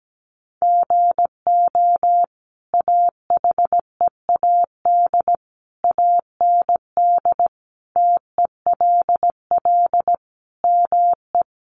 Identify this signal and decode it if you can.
Morse code, heard as a keyed tone, about 17 wpm, 700 Hz. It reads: GO AHEAD AND TELL ME
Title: Sentences from Top 1100 Words 17wpm